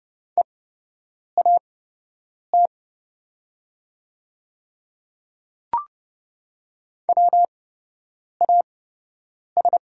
Morse code, heard as a keyed tone, about 30 wpm, spaced as for 8 wpm, 700 Hz. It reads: EAT WAS
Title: 8x Character Spacing - Top 300 Words - 30wpm